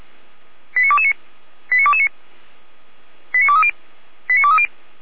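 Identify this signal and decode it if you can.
Five-tone selective call (ZVEI), heard as quick runs of five tones, tones of about 70 ms. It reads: 89229 89229 89230 89230
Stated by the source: ZVEI five-tone selective call received by SDR, hosted by feuerwehr-orlamuende.org (fire brigade Orlamünde).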